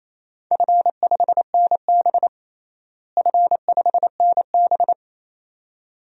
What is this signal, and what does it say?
Morse code, heard as a keyed tone, about 28 wpm, 700 Hz. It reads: F5NB F5NB
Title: DXCC-100 Call Signs - 4 Characters - Repeated 2x - 28wpm